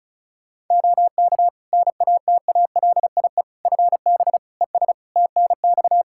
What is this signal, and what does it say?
Morse code, heard as a keyed tone, about 34 wpm, 700 Hz. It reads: OK NATALIE FB ES TNX
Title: Practice QSO - Exchange 2 - Info - 35wpm